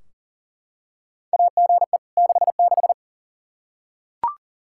Morse code, heard as a keyed tone, about 40 wpm, 700 Hz. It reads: AGE 66
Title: K1USN SST QSO Practice 40wpm